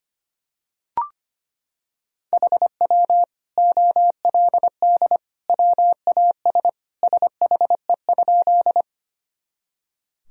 Morse code, heard as a keyed tone, about 25 wpm, 700 Hz. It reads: HW OLD WAS SHE?
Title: Sentences from Top 200 Words and Abbreviations 25wpm